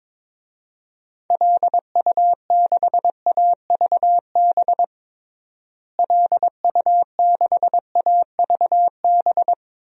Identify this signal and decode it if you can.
Morse code, heard as a keyed tone, about 22 wpm, 700 Hz. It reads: LU6AVB LU6AVB
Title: DXCC-100 Call Signs - 6 Characters - Repeated 3x - 22wpm